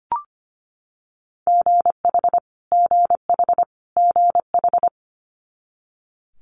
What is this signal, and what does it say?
Morse code, heard as a keyed tone, about 25 wpm, 700 Hz. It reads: GH GH GH
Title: Multi-letter Phonemes Repeated 3x 25wpm